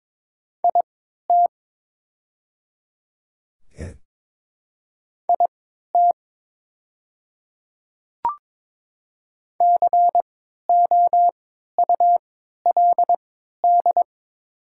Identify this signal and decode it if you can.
Morse code, heard as a keyed tone, about 22 wpm, spaced as for 13 wpm, 690 Hz. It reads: IT IT COULD
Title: 3x Character Spacing - Top 200 Words - 22wpm